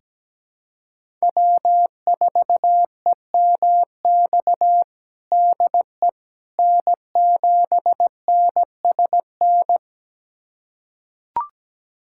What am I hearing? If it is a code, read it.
Morse code, heard as a keyed tone, about 17 wpm, 700 Hz. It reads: W4EMX DE N7NSN
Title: USA Callsign DE Callsign - 5 Characters - 17wpm